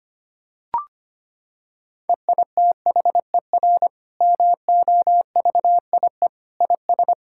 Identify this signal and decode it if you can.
Morse code, heard as a keyed tone, about 25 wpm, 700 Hz. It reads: EITHER MOVIE IS
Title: Sentences from Top 1000 Words - Review Part 5 - 25wpm